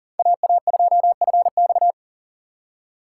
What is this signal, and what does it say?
Morse code, heard as a keyed tone, about 39 wpm, 700 Hz. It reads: AA2FX
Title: USA Call Signs - 5 Characters - 4x Character Spacing - 40wpm